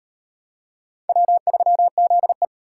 Morse code, heard as a keyed tone, about 38 wpm, 700 Hz. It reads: W3ZE